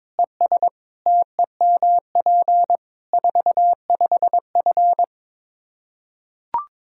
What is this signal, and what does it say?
Morse code, heard as a keyed tone, about 22 wpm, 700 Hz. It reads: ES TEMP 45F
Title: Practice QSO - Exchange 2 - Info - 22wpm